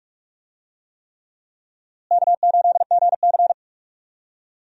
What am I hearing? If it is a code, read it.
Morse code, heard as a keyed tone, about 46 wpm, 700 Hz. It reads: K8GC
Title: USA Call Signs - 4 Characters - Repeated 3x - 45wpm